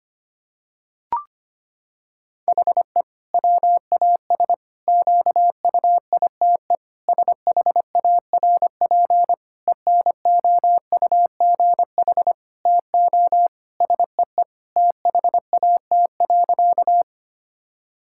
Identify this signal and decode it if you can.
Morse code, heard as a keyed tone, about 25 wpm, 700 Hz. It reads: HE WAS QUITE SHARP ENOUGH TO SEE THAT.
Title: Sentences from Top 1600 Words 25wpm